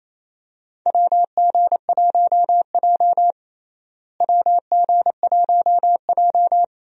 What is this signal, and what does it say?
Morse code, heard as a keyed tone, about 28 wpm, 700 Hz. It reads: WG1J WG1J
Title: USA Call Signs - 4 Characters - Repeated 3x - 28wpm